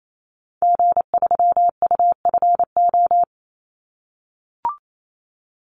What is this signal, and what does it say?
Morse code, heard as a keyed tone, about 28 wpm, 700 Hz. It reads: G3UFO
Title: Call Signs - 28wpm